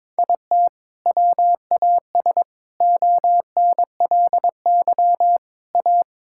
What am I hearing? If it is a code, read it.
Morse code, heard as a keyed tone, about 22 wpm, 700 Hz. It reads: IT WAS ONLY A
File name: Sets of 9 Words - 22wpm